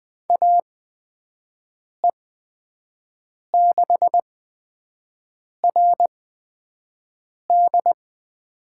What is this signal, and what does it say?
Morse code, heard as a keyed tone, about 20 wpm, spaced as for 5 wpm, 700 Hz. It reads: AE6RD
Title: USA Call Signs - 8x Character Spacing - 20wpm